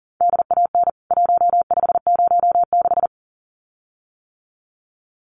Morse code, heard as a keyed tone, about 40 wpm, 700 Hz. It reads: DAN 1506
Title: CWT Exchange 40wpm